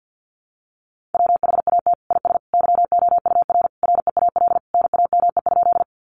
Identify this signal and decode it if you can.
Morse code, heard as a keyed tone, about 50 wpm, 700 Hz. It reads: WHAT IS YOUR REAL NAME?
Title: Sentences from Top 300 Words 50wpm